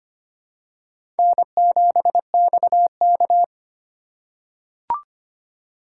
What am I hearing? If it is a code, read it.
Morse code, heard as a keyed tone, about 25 wpm, 700 Hz. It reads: N7XK
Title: USA Call Signs - 4x Character Spacing - 25wpm